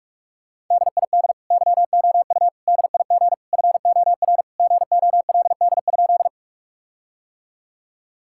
Morse code, heard as a keyed tone, about 41 wpm, 700 Hz. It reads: DID YOU DIG FOR GOLD?
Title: Sentences from Top 1600 Words 45wpm